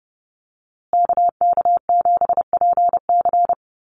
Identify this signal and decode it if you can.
Morse code, heard as a keyed tone, about 30 wpm, 700 Hz. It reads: KK7PC
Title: USA Call Signs Repeated 3x 30wpm